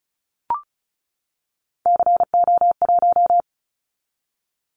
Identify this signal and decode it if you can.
Morse code, heard as a keyed tone, about 34 wpm, 700 Hz. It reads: CO1
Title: Callsign Prefixes for Top 100 DXCC - 35wpm